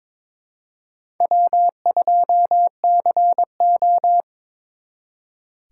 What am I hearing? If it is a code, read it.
Morse code, heard as a keyed tone, about 26 wpm, 700 Hz. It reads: W2CO
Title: Speed Racing - USA 1x2 Call Signs 22wpm